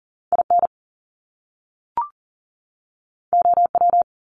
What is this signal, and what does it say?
Morse code, heard as a keyed tone, about 38 wpm, 700 Hz. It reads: IN OW
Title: Top Two Letter Combinations in Words 40wpm